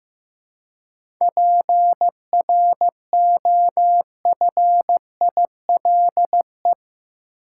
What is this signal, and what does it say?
Morse code, heard as a keyed tone, about 15 wpm, 700 Hz. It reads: PROFILE
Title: Seven Letter Words - 15wpm